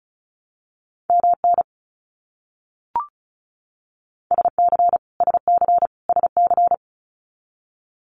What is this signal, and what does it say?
Morse code, heard as a keyed tone, about 34 wpm, 700 Hz. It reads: MN SC SC SC
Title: US States Repeated 3x 35wpm